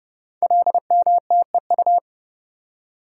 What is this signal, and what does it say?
Morse code, heard as a keyed tone, about 28 wpm, 700 Hz. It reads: LMTEU
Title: Code Groups - 5 Letters - 30wpm